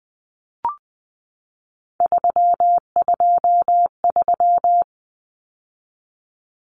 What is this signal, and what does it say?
Morse code, heard as a keyed tone, about 20 wpm, 700 Hz. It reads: 323
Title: Common Cut Numbers - 3-Digit 20wpm